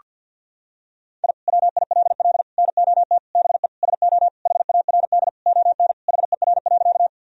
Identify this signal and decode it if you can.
Morse code, heard as a keyed tone, about 50 wpm, 700 Hz. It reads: I WILL NOT BE SO HARD ON HER.
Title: Sentences from Top 1000 Words - Review Part 1 - 50wpm